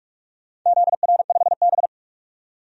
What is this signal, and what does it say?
Morse code, heard as a keyed tone, about 47 wpm, 700 Hz. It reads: ZR5B
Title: DXCC-100 Call Signs - 4 Characters - 45wpm